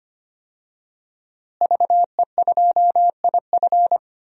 Morse code, heard as a keyed tone, about 25 wpm, 700 Hz. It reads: VE2IF
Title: DXCC-100 Call Signs - 5 Characters - Repeated 3x - 25wpm